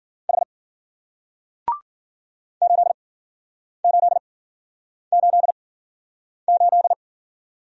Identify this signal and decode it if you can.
Morse code, heard as a keyed tone, about 52 wpm, spaced as for 9 wpm, 700 Hz. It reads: H 8888